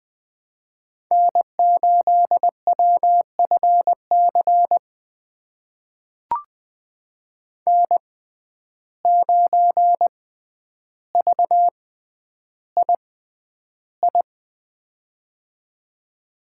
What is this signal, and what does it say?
Morse code, heard as a keyed tone, about 20 wpm, spaced as for 10 wpm, 700 Hz. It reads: N8WFC N9VII
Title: USA Call Signs - 5 Characters - 6x Character Spacing - 20wpm